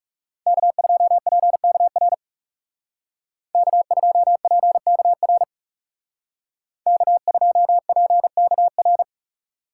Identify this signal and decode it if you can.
Morse code, heard as a keyed tone, about 39 wpm, 700 Hz. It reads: K2PKR K2PKR K2PKR